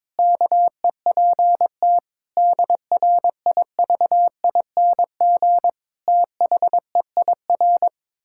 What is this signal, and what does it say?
Morse code, heard as a keyed tone, about 22 wpm, 700 Hz. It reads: KEPT DRIVING THEIR
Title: Pangram Challenge v2 - 22wpm